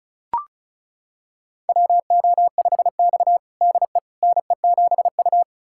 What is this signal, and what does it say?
Morse code, heard as a keyed tone, about 36 wpm, 690 Hz. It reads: WO5X DE NE7U
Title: USA Callsign DE Callsign - 4 Characters - 35wpm